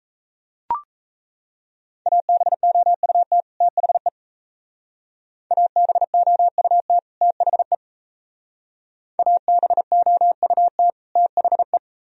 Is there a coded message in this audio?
Yes: Morse code from 2.1 to 11.8 s, heard as a keyed tone, about 35 wpm, 700 Hz. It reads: ABOUT THE ABOUT THE ABOUT THE